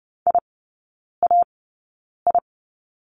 Morse code, heard as a keyed tone, about 30 wpm, spaced as for 9 wpm, 700 Hz. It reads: IAI
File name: course / 30:10wpm / 087 Lesson 15e - Number 5 - Three Characters 30wpm